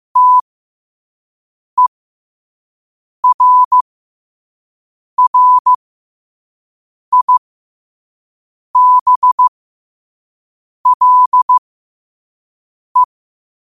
Morse code, heard as a keyed tone, about 15 wpm, spaced as for 5 wpm, 1000 Hz. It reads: TERRIBLE
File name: qotdfiles / 5wpm